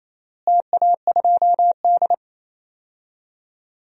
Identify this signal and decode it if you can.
Morse code, heard as a keyed tone, about 28 wpm, 700 Hz. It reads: TA2D
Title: DXCC-100 Call Signs - 4 Characters - Repeated 2x - 28wpm